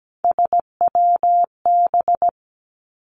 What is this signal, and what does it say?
Morse code, heard as a keyed tone, about 17 wpm, 700 Hz. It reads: SWB
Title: Three Challenging Letter-Number Combinations 17wpm